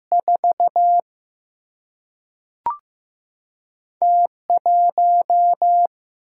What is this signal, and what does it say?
Morse code, heard as a keyed tone, about 15 wpm, 700 Hz. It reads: 4 T 1
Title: Single-Letter Single-Number 15wpm